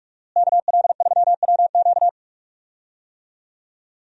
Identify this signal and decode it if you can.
Morse code, heard as a keyed tone, about 43 wpm, 700 Hz. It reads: KL3WQ
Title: USA Call Signs - 5 Characters - Repeated 3x - 45wpm